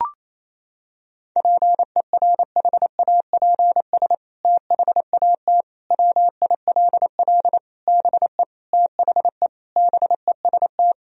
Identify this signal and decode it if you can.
Morse code, heard as a keyed tone, about 28 wpm, 700 Hz. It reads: PERHAPS THAT WILL BE THE BEST
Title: Sentences from Top 1000 Words - Review Part 2 - 28wpm